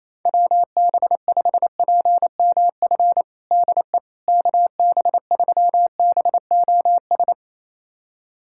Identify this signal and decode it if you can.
Morse code, heard as a keyed tone, about 28 wpm, 700 Hz. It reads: WB5PMF DE KB3BOS
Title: USA Callsign DE Callsign 28wpm